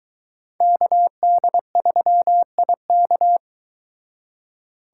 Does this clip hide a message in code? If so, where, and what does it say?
Yes: Morse code from 0.6 to 3.4 s, heard as a keyed tone, about 22 wpm, 700 Hz. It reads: KD3IK